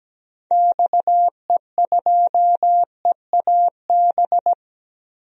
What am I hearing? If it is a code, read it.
Morse code, heard as a keyed tone, about 17 wpm, 700 Hz. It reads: XE2EAB
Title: DXCC-100 Call Signs - 6 Characters - Repeated 3x - 17wpm